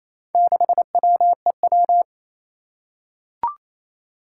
Morse code, heard as a keyed tone, about 28 wpm, 700 Hz. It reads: 6WEW
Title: Four Challenging Letter-Number Combinations 28wpm